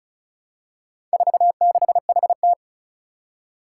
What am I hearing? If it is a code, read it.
Morse code, heard as a keyed tone, about 36 wpm, 700 Hz. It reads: 46HT
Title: Four Challenging Letter-Number Combinations 35wpm